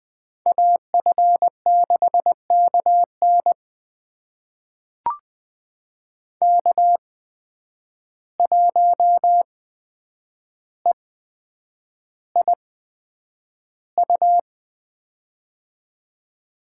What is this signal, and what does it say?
Morse code, heard as a keyed tone, about 21 wpm, spaced as for 9 wpm, 700 Hz. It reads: AF6KN K1EIU
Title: USA Call Signs - 5 Characters - 8x Character Spacing - 20wpm